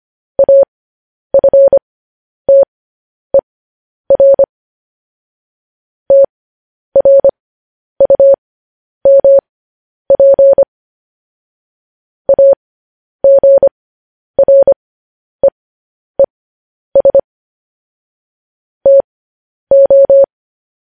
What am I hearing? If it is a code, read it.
Morse code, heard as a keyed tone, about 25 wpm, spaced as for 10 wpm, 550 Hz. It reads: AFTER TRUMP AGREES TO
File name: podcast / News headlines-1x-25x10